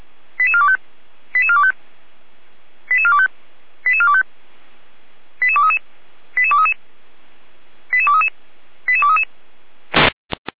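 Five-tone selective call (ZVEI), heard as quick runs of five tones, tones of about 70 ms. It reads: 88425 88425 88425 88425 88233 88233 88233 88233
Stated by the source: ZVEI five-tone selective call 88413 received with an SDR.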